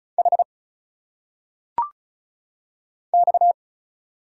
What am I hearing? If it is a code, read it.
Morse code, heard as a keyed tone, about 33 wpm, 700 Hz. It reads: H X
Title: Single Letters - Phonetic Pronunciation 35wpm